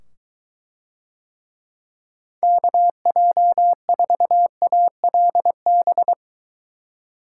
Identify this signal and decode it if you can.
Morse code, heard as a keyed tone, about 23 wpm, 700 Hz. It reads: KJ4ALB